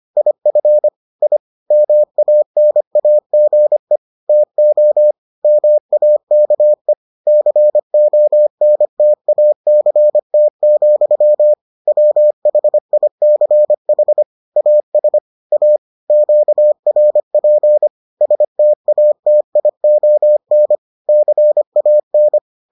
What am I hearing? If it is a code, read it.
Morse code, heard as a keyed tone, about 25 wpm, 600 Hz. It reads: IF I MANAGE TO MAKE CONTACT, WHICH AS A QRP STATION CAN